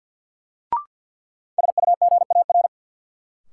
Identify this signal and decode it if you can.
Morse code, heard as a keyed tone, about 48 wpm, 700 Hz. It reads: SUGAR
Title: Sentences from Top 1200 Words - New Word Drill Alt - 50wpm